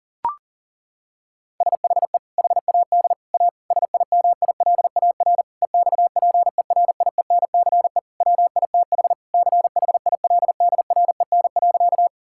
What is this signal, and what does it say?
Morse code, heard as a keyed tone, about 40 wpm, 700 Hz. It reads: SHE HAD A SIMILAR EXPERIENCE WITH CHILDREN.
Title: Sentences from Top 1000 Words - Review Part 5 - 40wpm